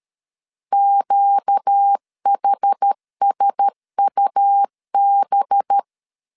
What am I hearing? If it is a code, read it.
Morse code, heard as a keyed tone, about 13 wpm, 800 Hz. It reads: QHSUB
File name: CW / DET / 1 6 1500C S A 12wpm 3440